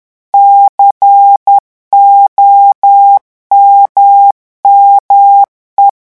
Morse code, heard as a keyed tone, about 10 wpm, 800 Hz. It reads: COMME
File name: revision9 citations humoristiques vitesse 10 mots minute